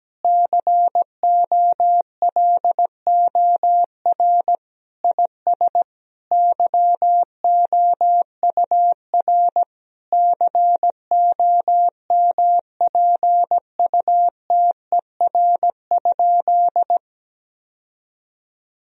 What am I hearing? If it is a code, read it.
Morse code, heard as a keyed tone, about 17 wpm, 700 Hz. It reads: COLOR IS YOUR COMPUTER?
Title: Sentences from Top 1000 Words - Review Part 3 - 17wpm